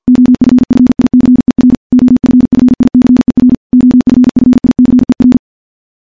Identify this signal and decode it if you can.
Morse code, heard as a keyed tone, about 50 wpm, 250 Hz. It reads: OFFICER OFFICER OFFICER
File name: morse / morse-training-50WPM